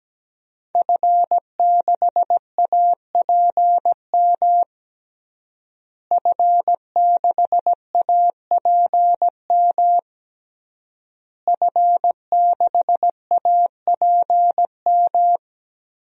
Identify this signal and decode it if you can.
Morse code, heard as a keyed tone, about 17 wpm, 700 Hz. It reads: F6APM F6APM F6APM
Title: DXCC-100 Call Signs - 5 Characters - Repeated 3x - 17wpm